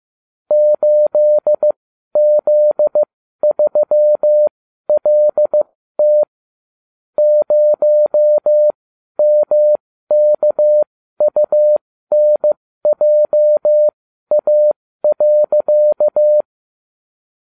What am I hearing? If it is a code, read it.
Morse code, heard as a keyed tone, about 15 wpm, spaced as for 12 wpm, 600 Hz. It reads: 8Z3LT 0MKUNJA.